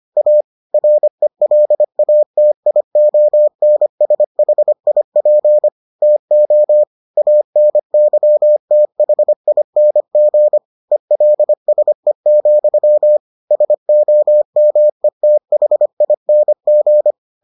Morse code, heard as a keyed tone, about 25 wpm, 600 Hz. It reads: A RELATIONSHIP TO ANYTHING ELSE, SOMETHING